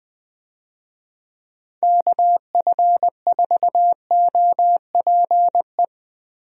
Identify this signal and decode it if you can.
Morse code, heard as a keyed tone, about 21 wpm, 700 Hz. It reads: KF4OPE
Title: USA Call Signs - 6 Characters - Repeated 2x - 20wpm